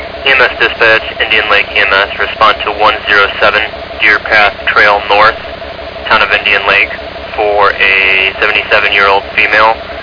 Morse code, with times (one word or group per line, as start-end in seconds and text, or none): none